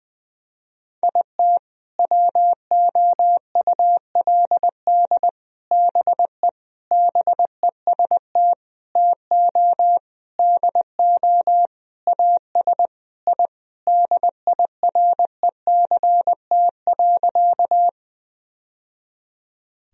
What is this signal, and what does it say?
Morse code, heard as a keyed tone, about 20 wpm, 700 Hz. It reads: IT WOULD BE BEST TO DO AS I DIRECT.
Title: Sentences from Top 1000 Words - Review Part 2 - 20wpm